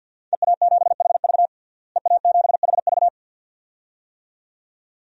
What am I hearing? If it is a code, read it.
Morse code, heard as a keyed tone, about 52 wpm, 700 Hz. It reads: EA7HV EA7HV
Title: DXCC-100 Call Signs - 5 Characters - Repeated 2x - 50wpm